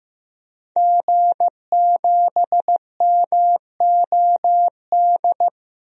Morse code, heard as a keyed tone, about 15 wpm, 700 Hz. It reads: G7MOD
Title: DXCC-100 Call Signs - 5 Characters - Repeated 3x - 15wpm